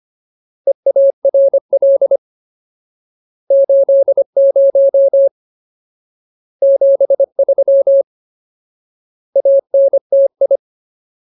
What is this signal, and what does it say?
Morse code, heard as a keyed tone, about 25 wpm, 550 Hz. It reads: EARL 80 73 ANTI
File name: Ultimate Word List / 25WPM Ultimate